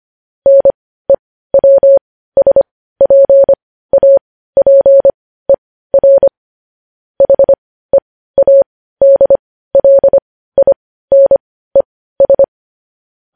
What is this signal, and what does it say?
Morse code, heard as a keyed tone, about 25 wpm, spaced as for 15 wpm, 550 Hz. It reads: NEWSPAPER HEADLINES
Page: news-headlines-in-morse-code-at-15-wpm